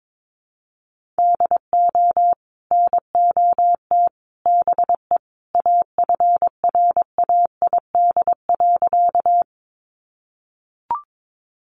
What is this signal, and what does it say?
Morse code, heard as a keyed tone, about 22 wpm, 700 Hz. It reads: DO NOT BE AFRAID.
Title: Sentences from Top 1500 Words 22wpm